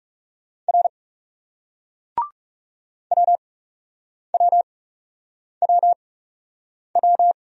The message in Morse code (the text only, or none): R WWWW